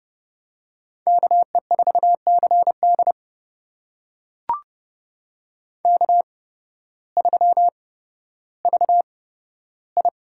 Morse code, heard as a keyed tone, about 32 wpm, spaced as for 14 wpm, 700 Hz. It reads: KE4CD K3VI
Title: USA Call Signs - 8x Character Spacing - 30wpm